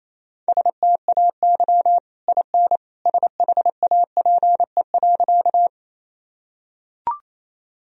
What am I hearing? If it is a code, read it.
Morse code, heard as a keyed tone, about 28 wpm, 700 Hz. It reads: STAY IN SHAPE.
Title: Sentences from Top 1000 Words - Review Part 3 - 28wpm